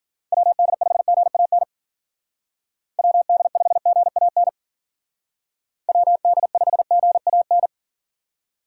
Morse code, heard as a keyed tone, about 50 wpm, 700 Hz. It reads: WD5GAN WD5GAN WD5GAN
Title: Speed Racing - USA 2x3 Call Signs 40wpm